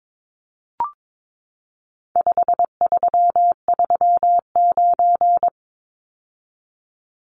Morse code, heard as a keyed tone, about 22 wpm, 700 Hz. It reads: 5339
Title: Numbers - 4-Digit 22wpm